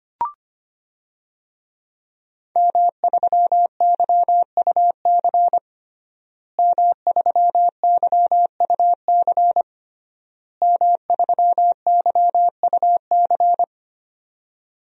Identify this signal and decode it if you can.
Morse code, heard as a keyed tone, about 25 wpm, 700 Hz. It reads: M3YUC M3YUC M3YUC
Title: DXCC-100 Call Signs - 5 Characters - Repeated 3x - 25wpm